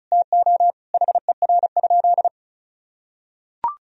Morse code, heard as a keyed tone, about 36 wpm, 700 Hz. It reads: TO HER?
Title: Sentences from Top 1000 Words - Review Part 4 - 35wpm